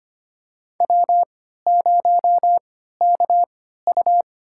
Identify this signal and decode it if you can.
Morse code, heard as a keyed tone, about 25 wpm, spaced as for 14 wpm, 700 Hz. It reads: W0KU
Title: USA Call Signs - 4 Characters - 3x Character Spacing - 25wpm